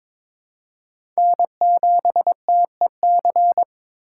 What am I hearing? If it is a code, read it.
Morse code, heard as a keyed tone, about 22 wpm, 700 Hz. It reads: N7TEC
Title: USA Call Signs - 8x Character Spacing - 22wpm